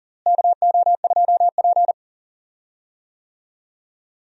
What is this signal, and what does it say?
Morse code, heard as a keyed tone, about 40 wpm, 700 Hz. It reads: KO2P
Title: USA Call Signs - 4 Characters - Repeated 2x - 40wpm